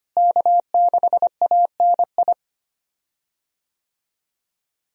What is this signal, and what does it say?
Morse code, heard as a keyed tone, about 25 wpm, 700 Hz. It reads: K6ANI
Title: USA Call Signs - 5 Characters - Repeated 3x - 25wpm